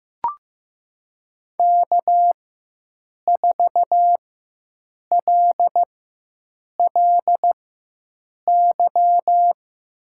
Morse code, heard as a keyed tone, about 15 wpm, spaced as for 7 wpm, 700 Hz. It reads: K4LLY
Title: USA Call Signs - 5 Characters - 4x Character Spacing - 15wpm